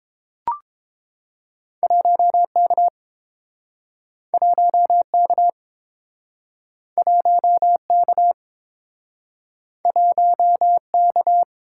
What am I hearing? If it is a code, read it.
Morse code, heard as a keyed tone, about 30 wpm, 700 Hz. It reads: 1K 1K 1K 1K